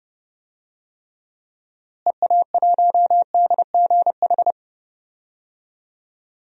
Morse code, heard as a keyed tone, about 32 wpm, 700 Hz. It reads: EA1DGH
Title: DXCC-100 Call Signs - 6 Characters - Repeated 3x - 30wpm